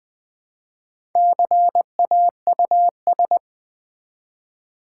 Morse code, heard as a keyed tone, about 20 wpm, 700 Hz. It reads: CAUS
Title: Build A Word - 1 Letter At A Time - 5-Letter Words - 20wpm